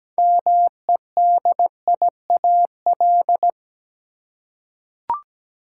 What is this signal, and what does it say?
Morse code, heard as a keyed tone, about 17 wpm, 700 Hz. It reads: MEDIAL
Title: Practice QSO - Exchange 3 - More Info - 17wpm